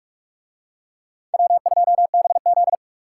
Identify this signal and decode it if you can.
Morse code, heard as a keyed tone, about 44 wpm, 700 Hz. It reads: W2BZ